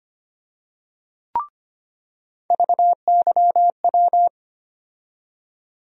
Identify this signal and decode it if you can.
Morse code, heard as a keyed tone, about 25 wpm, 700 Hz. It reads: VYW
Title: USA Call Sign Postfixes - 3 Letters 25wpm